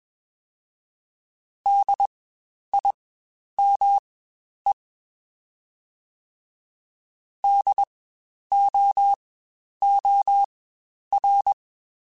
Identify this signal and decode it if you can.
Morse code, heard as a keyed tone, about 21 wpm, spaced as for 10 wpm, 800 Hz. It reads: DIME DOOR
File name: Spaced Alphas / Spaced Alphas mp3s / Ds sp @20